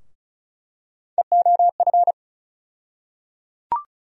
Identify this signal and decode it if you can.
Morse code, heard as a keyed tone, about 34 wpm, 700 Hz. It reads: EOF
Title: USA Call Sign Postfixes - 3 Letters 35wpm